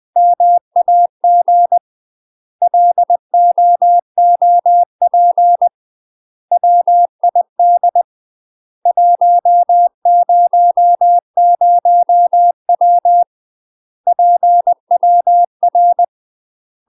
Morse code, heard as a keyed tone, about 20 wpm, 700 Hz. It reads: MAG LOOP WID 100W PWR
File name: UK MASTER QSOs TRAINING / 20WPM / UK MASTER QSOs 0051